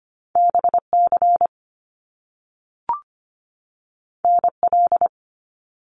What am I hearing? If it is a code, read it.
Morse code, heard as a keyed tone, about 25 wpm, 700 Hz. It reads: BC NL
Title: Canadian Provinces and Territories 25wpm